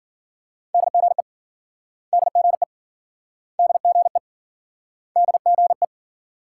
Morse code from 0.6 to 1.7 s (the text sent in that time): DGE